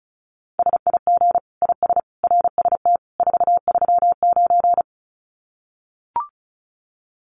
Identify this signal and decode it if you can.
Morse code, heard as a keyed tone, about 34 wpm, 700 Hz. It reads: SIG IS RST 439
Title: Common QSO Phrases 35wpm